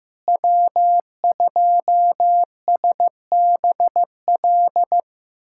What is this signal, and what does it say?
Morse code, heard as a keyed tone, about 15 wpm, 700 Hz. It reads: W2SBL
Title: USA Callsign DE Callsign - 5 Characters - 15wpm